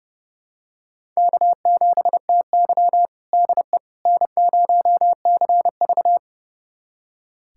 Morse code, heard as a keyed tone, about 30 wpm, 700 Hz. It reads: K7TY DE N0CV
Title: USA Callsign DE Callsign - 4 Characters - 30wpm